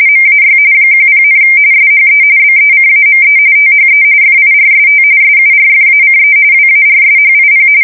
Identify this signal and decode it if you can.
RTTY, about 45.45 baud. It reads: MAY 10, SHE BECAME THE FIRST SURFACE WARSHIP
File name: fsk170 / USS-New-Jersey